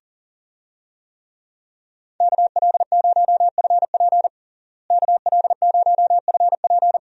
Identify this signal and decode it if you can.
Morse code, heard as a keyed tone, about 40 wpm, 700 Hz. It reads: KL0FP KL0FP
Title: USA Call Signs - 5 Characters - Repeated 2x - 40wpm